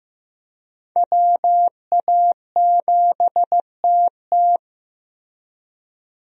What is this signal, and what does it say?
Morse code, heard as a keyed tone, about 15 wpm, 700 Hz. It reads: WA7TT
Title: USA Call Signs - 5 Characters - Repeated 3x - 15wpm